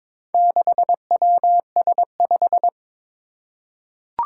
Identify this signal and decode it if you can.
Morse code, heard as a keyed tone, about 22 wpm, 700 Hz. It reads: 6WS5
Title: Four Challenging Letter-Number Combinations 22wpm